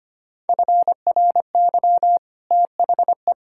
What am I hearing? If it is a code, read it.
Morse code, heard as a keyed tone, about 25 wpm, 700 Hz. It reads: FRY THE
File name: Easy Sentences with Hints - 25wpm